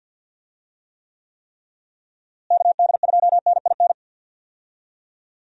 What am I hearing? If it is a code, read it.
Morse code, heard as a keyed tone, about 51 wpm, 700 Hz. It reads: KD2NIN
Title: USA Call Signs - 6 Characters - Repeated 3x - 50wpm